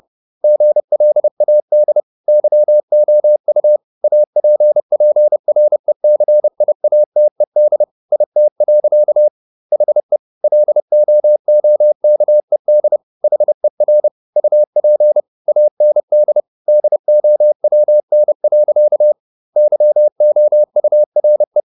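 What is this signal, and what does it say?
Morse code, heard as a keyed tone, about 30 wpm, 600 Hz. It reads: GLAD YOU APPRECIATED IT. HE LOOKED HER UP AND DOWN. YOURE
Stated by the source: Morse code at 30 wpm, the novel Never by Ken Follett in English, from CW med Gnister.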